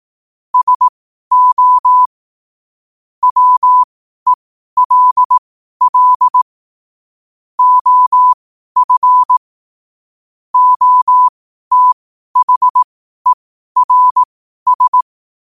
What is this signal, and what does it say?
Morse code, heard as a keyed tone, about 18 wpm, spaced as for 12 wpm, 1000 Hz. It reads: SO WELL OF OTHERS
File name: qotdfiles / 12wpm